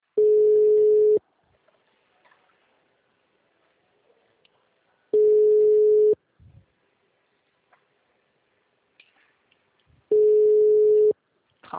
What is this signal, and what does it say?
Ringback tone (Europe, 425 Hz).